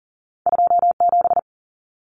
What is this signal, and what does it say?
Morse code, heard as a keyed tone, about 40 wpm, 700 Hz. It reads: 27
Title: Common Cut Numbers - 2-Digit 40wpm